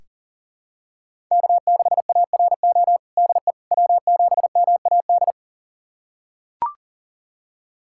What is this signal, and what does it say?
Morse code, heard as a keyed tone, about 40 wpm, 700 Hz. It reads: K6ARO DE W7MAD
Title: USA Callsign DE Callsign - 5 Characters - 40wpm